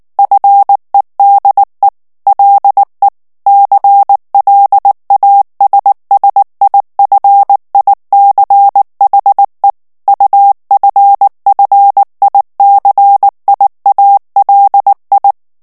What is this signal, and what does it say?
Morse code, heard as a keyed tone, about 19 wpm, 800 Hz. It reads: FEDE LE CLASSIFICHE UFFICIALI
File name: morse mp3 / ITALIEN texte clair 2 vitesse 18 mots minute